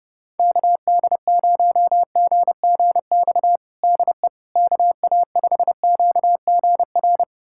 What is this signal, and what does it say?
Morse code, heard as a keyed tone, about 30 wpm, 700 Hz. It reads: KD0GGX DE KA5QGR
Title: USA Callsign DE Callsign 30wpm